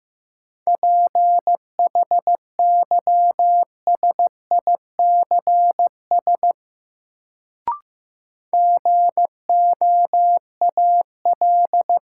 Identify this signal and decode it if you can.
Morse code, heard as a keyed tone, about 15 wpm, 700 Hz. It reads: PHYSICS GOAL
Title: Sentences from Top 1000 Words - Review Part 5 - New Word Drill - 15wpm